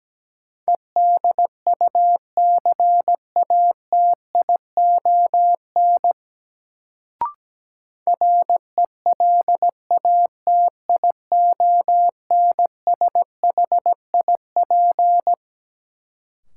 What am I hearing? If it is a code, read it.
Morse code, heard as a keyed tone, about 17 wpm, 700 Hz. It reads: EDUCATION RELATIONSHIP
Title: Sentences from Top 1100 Words - New Word Drill Alt - 17wpm